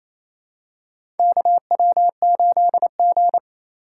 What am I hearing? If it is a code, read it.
Morse code, heard as a keyed tone, about 28 wpm, 700 Hz. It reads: KW8G
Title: USA Call Signs - 4 Characters - 3x Character Spacing - 28wpm